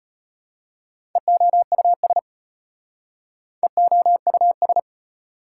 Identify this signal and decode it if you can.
Morse code, heard as a keyed tone, about 36 wpm, 700 Hz. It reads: EOUS EOUS